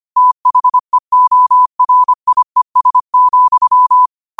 Morse code, heard as a keyed tone, about 25 wpm, 1000 Hz. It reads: THEORIES,
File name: qotdfiles / 25wpm